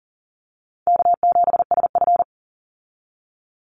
Morse code, heard as a keyed tone, about 40 wpm, 700 Hz. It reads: K7SF
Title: Easy Call Signs - 40wpm